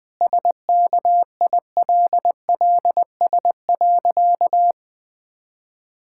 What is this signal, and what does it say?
Morse code, heard as a keyed tone, about 20 wpm, 700 Hz. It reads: SKILLS.
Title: Sentences with 2-Word Phrases - Spoken Beginning - 20wpm